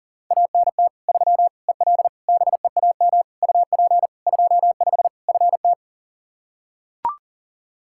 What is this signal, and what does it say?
Morse code, heard as a keyed tone, about 40 wpm, 700 Hz. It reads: ANT 3 EL BEAM UP 25 FT
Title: Practice QSO - Exchange 2 - Info - 40wpm